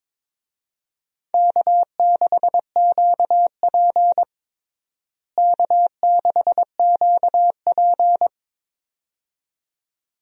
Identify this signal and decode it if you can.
Morse code, heard as a keyed tone, about 22 wpm, 700 Hz. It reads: K6QP K6QP
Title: USA Call Signs - 4 Characters - Repeated 2x - 22wpm